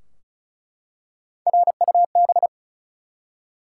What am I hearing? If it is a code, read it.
Morse code, heard as a keyed tone, about 34 wpm, 700 Hz. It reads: RUB
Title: Three Letter Words - 35wpm